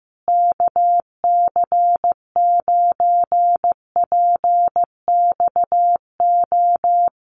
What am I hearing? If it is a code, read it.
Morse code, heard as a keyed tone, about 15 wpm, 700 Hz. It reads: KC9PXO
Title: USA Call Signs Repeated 3x 15wpm